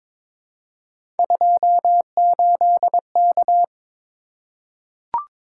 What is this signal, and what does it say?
Morse code, heard as a keyed tone, about 22 wpm, 700 Hz. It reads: 28K
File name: practice / 22wpm / Three Letter-Number Combinations - 22wpm